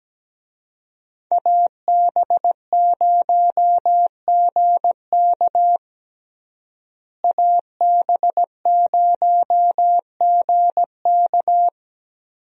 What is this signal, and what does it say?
Morse code, heard as a keyed tone, about 17 wpm, 700 Hz. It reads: AB0GK AB0GK
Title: USA Call Signs - 5 Characters - Repeated 2x - 17wpm